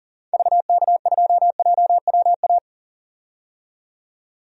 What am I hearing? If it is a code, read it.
Morse code, heard as a keyed tone, about 40 wpm, 700 Hz. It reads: VK2JWA
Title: DXCC-100 Call Signs - 6 Characters - 40wpm